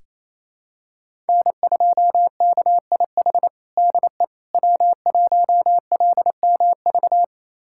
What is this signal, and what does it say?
Morse code, heard as a keyed tone, about 28 wpm, 700 Hz. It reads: N2KIH DE W1LMV
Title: USA Callsign DE Callsign - 5 Characters - 28wpm